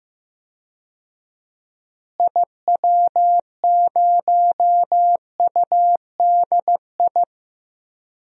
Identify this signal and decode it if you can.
Morse code, heard as a keyed tone, about 15 wpm, 700 Hz. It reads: IW0UDI
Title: DXCC-100 Call Signs - 6 Characters - Repeated 2x - 15wpm